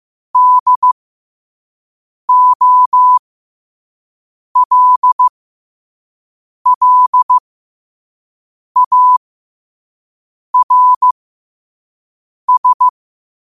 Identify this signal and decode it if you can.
Morse code, heard as a keyed tone, about 15 wpm, spaced as for 5 wpm, 1000 Hz. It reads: DOLLARS
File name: qotdfiles / 5wpm